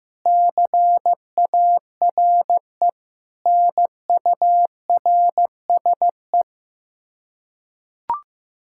Morse code, heard as a keyed tone, about 15 wpm, 700 Hz. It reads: CARE NURSE
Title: Practice QSO - Exchange 3 - More Info - 15wpm